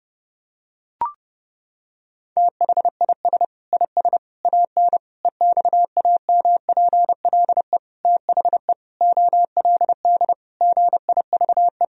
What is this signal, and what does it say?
Morse code, heard as a keyed tone, about 30 wpm, 700 Hz. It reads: THIS IS AN EXAMPLE THE OLD GIVE
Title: Sentences from Top 1000 Words - Review Part 1 - 30wpm